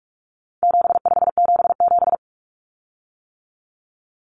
Morse code, heard as a keyed tone, about 44 wpm, 700 Hz. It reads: 7577
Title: Common Cut Numbers - 4-Digit 45wpm